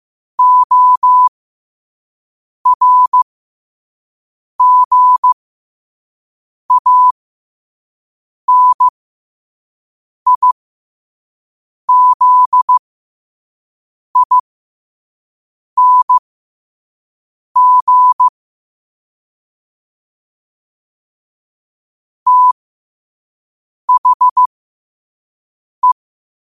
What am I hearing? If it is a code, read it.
Morse code, heard as a keyed tone, about 15 wpm, spaced as for 5 wpm, 1000 Hz. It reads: ORGANIZING THE
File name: qotdfiles / 5wpm